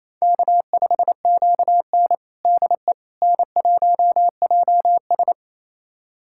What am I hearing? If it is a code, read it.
Morse code, heard as a keyed tone, about 28 wpm, 700 Hz. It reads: K5QN DE N1JS
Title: USA Callsign DE Callsign - 4 Characters - 28wpm